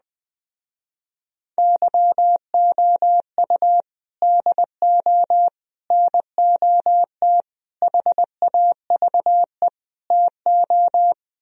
Morse code, heard as a keyed tone, about 20 wpm, 700 Hz. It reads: YOU DO NOT HAVE TO